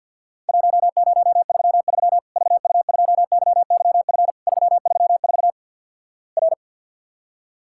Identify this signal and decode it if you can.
Morse code, heard as a keyed tone, about 50 wpm, 700 Hz. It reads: 1033 VU2YYF 334 R
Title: Worked All Europe DX Contest - QTC Exchange - 50wpm